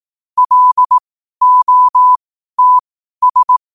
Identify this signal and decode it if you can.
Morse code, heard as a keyed tone, about 18 wpm, spaced as for 12 wpm, 1000 Hz. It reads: LOTS
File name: qotdfiles / 12wpm